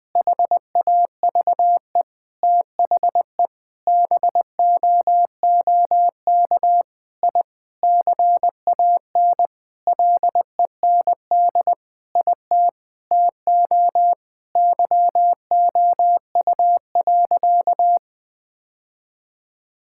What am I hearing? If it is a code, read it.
Morse code, heard as a keyed tone, about 20 wpm, 700 Hz. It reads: HAVE THE BOOK I CAN LEND IT TO YOU.
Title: Sentences with 5-Word Phrases - 20wpm